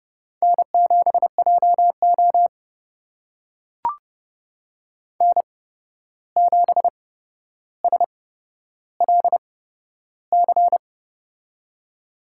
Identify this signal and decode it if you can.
Morse code, heard as a keyed tone, about 32 wpm, spaced as for 12 wpm, 700 Hz. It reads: N7JO N7SLC
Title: USA Call Signs - 8x Character Spacing - 30wpm